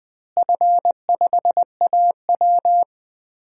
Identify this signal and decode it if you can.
Morse code, heard as a keyed tone, about 20 wpm, 700 Hz. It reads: F5AW
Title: DXCC-100 Call Signs - 4 Characters - Repeated 3x - 20wpm